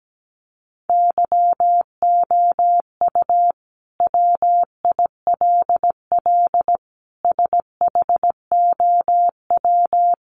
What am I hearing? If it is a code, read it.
Morse code, heard as a keyed tone, about 17 wpm, 700 Hz. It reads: YOU WILL SHOW
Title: Sentences from Top 600 Words 17wpm